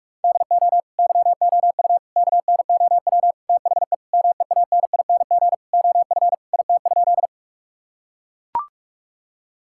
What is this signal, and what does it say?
Morse code, heard as a keyed tone, about 41 wpm, 700 Hz. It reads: DO YOU KNOW THE MEANING OF IT?
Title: Sentences from Top 1000 Words - Review Part 6 - 45wpm